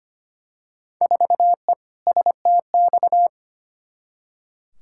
Morse code, heard as a keyed tone, about 25 wpm, 700 Hz. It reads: 4E STX
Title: Field Day Exchanges Encore - 25wpm